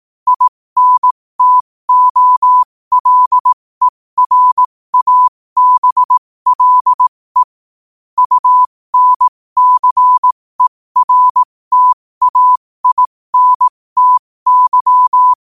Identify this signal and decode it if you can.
Morse code, heard as a keyed tone, about 18 wpm, spaced as for 15 wpm, 1000 Hz. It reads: INTOLERABLE UNCERTAINTY
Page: qotd-at-15-wpm